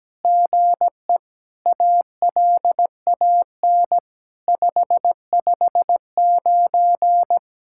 Morse code, heard as a keyed tone, about 17 wpm, 700 Hz. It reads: GE ALAN 559
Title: SOTA QSO Practice 17wpm